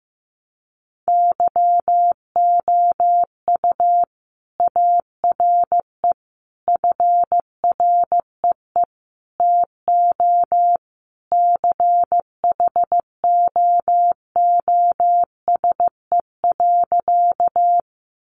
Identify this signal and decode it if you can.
Morse code, heard as a keyed tone, about 15 wpm, 700 Hz. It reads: YOU ARE FREE TO CHOOSE.
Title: Sentences from Top 700 Words 15wpm